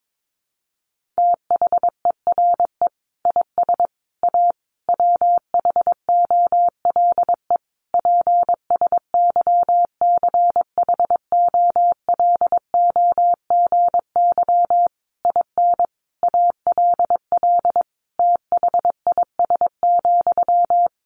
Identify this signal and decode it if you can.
Morse code, heard as a keyed tone, about 22 wpm, 700 Hz. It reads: THERE IS A WHOLE PSYCHOLOGY IN ALL THIS,
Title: Sentences from Top 1400 Words 22wpm